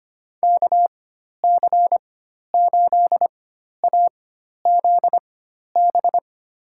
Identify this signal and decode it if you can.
Morse code, heard as a keyed tone, about 25 wpm, spaced as for 12 wpm, 700 Hz. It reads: KC8AZB
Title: USA Call Signs - 6 Characters - 4x Character Spacing - 25wpm